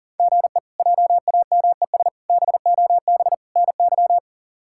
Morse code, heard as a keyed tone, about 40 wpm, 700 Hz. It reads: GE JAMES BOB NY
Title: K1USN SST QSO Practice - 40wpm